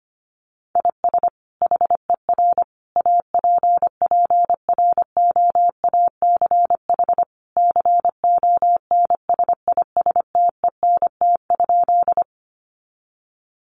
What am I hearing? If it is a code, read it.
Morse code, heard as a keyed tone, about 25 wpm, 700 Hz. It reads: IS HER APPROACH CONSISTENT?
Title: Sentences from Top 1500 Words 25wpm